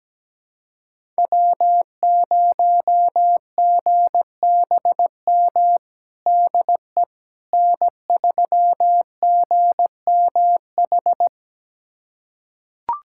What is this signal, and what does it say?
Morse code, heard as a keyed tone, about 17 wpm, 700 Hz. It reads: W0GBM DE N3GMH
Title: USA Callsign DE Callsign - 5 Characters - 17wpm